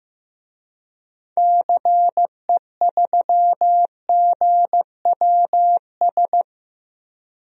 Morse code, heard as a keyed tone, about 15 wpm, 700 Hz. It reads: CE3GWS
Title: DXCC-100 Call Signs - 6 Characters - 15wpm